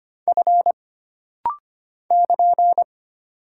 Morse code, heard as a keyed tone, about 25 wpm, 700 Hz. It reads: F <KN>
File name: Single Character (Letter:Number:Punctuation:Prosign) - Rapid-Fire - 25wpm